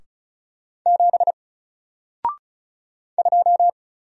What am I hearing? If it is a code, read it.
Morse code, heard as a keyed tone, about 36 wpm, 700 Hz. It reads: 7 2
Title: Single Character (Letter:Number:Punctuation:Prosign) Alt 35wpm